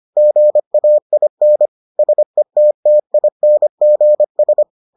Morse code, heard as a keyed tone, about 25 wpm, 600 Hz. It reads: GAIN SETTINGS